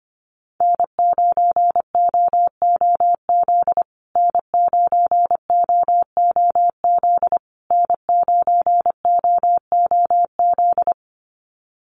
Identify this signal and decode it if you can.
Morse code, heard as a keyed tone, about 25 wpm, 700 Hz. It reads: N9OOZ N9OOZ N9OOZ
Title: USA Call Signs Repeated 3x 25wpm